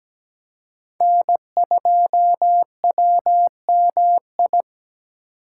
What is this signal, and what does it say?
Morse code, heard as a keyed tone, about 17 wpm, 700 Hz. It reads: N2WMI
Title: USA Call Signs - 5 Characters - 2x Character Spacing - 17wpm